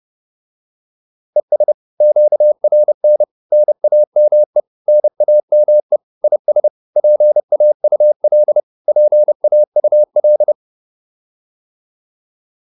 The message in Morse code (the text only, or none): ES QRN NAME NAME IS PAUL PAUL